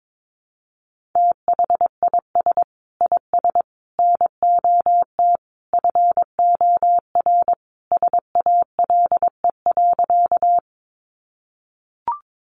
Morse code, heard as a keyed tone, about 22 wpm, 700 Hz. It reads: THIS IS NOT FOR SALE.
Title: Sentences from Top 1400 Words 22wpm